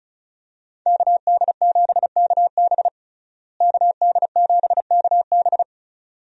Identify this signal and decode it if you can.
Morse code, heard as a keyed tone, about 36 wpm, 700 Hz. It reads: KD7KB KD7KB
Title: USA Call Signs - 5 Characters - Repeated 2x - 35wpm